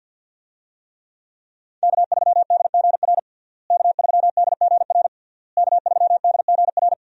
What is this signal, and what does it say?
Morse code, heard as a keyed tone, about 52 wpm, 700 Hz. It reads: K3DGR K3DGR K3DGR
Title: USA Call Signs - 5 Characters - Repeated 3x - 50wpm